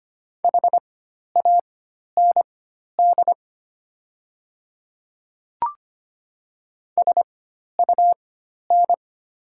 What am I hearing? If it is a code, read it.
Morse code, heard as a keyed tone, about 25 wpm, spaced as for 12 wpm, 700 Hz. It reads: HAND SUN
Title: 4x Character Spacing - Top 300 Words - 25wpm